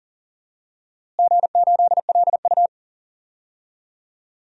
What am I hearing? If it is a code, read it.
Morse code, heard as a keyed tone, about 37 wpm, 700 Hz. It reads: G8LU
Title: DXCC-100 Call Signs - 4 Characters - Repeated 3x - 40wpm